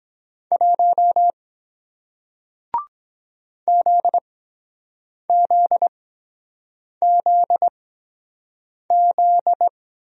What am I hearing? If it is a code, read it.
Morse code, heard as a keyed tone, about 23 wpm, spaced as for 7 wpm, 700 Hz. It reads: 1 ZZZZ